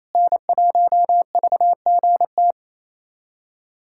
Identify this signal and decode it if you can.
Morse code, heard as a keyed tone, about 27 wpm, 700 Hz. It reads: N1VGT
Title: USA Callsign DE Callsign - 5 Characters - 28wpm